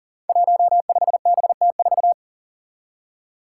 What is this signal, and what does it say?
Morse code, heard as a keyed tone, about 40 wpm, 700 Hz. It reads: 15BT4
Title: Five Challenging Letter-Number Combinations 40wpm